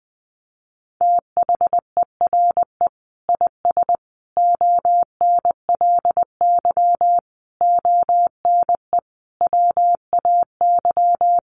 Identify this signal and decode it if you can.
Morse code, heard as a keyed tone, about 20 wpm, 700 Hz. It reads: THERE IS ONLY ONE WAY
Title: Sentences from Top 1200 Words 20wpm